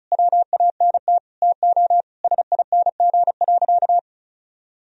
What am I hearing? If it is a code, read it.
Morse code, heard as a keyed tone, about 36 wpm, 700 Hz. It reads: WANT TO SING.
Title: Sentences from Top 1000 Words - Review Part 2 - 35wpm